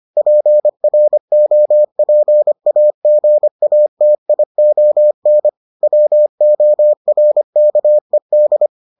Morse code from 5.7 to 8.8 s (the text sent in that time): WORKED